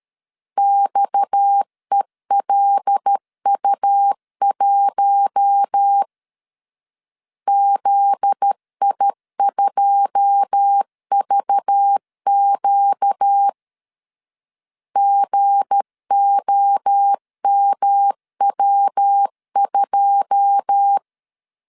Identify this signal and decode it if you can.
Morse code, heard as a keyed tone, about 13 wpm, 800 Hz. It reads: XELU1 ZI2VQ GOMW2
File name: LEZIONI / 9 2 12wpm 2301